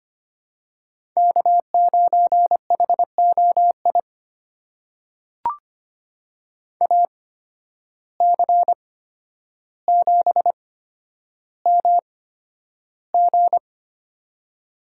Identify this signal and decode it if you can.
Morse code, heard as a keyed tone, about 25 wpm, spaced as for 11 wpm, 700 Hz. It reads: K9HOI AC7MG
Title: USA Call Signs - 5 Characters - 8x Character Spacing - 25wpm